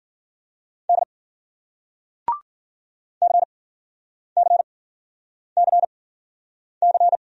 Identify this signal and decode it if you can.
Morse code, heard as a keyed tone, about 51 wpm, spaced as for 9 wpm, 700 Hz. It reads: D CCCC